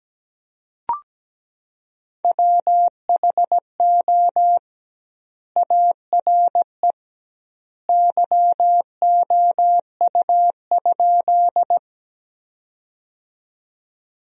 Morse code, heard as a keyed tone, about 17 wpm, 700 Hz. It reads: WHO ARE YOU?
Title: 2x Word-Spacing - Sentences from Top 100 Words - 17wpm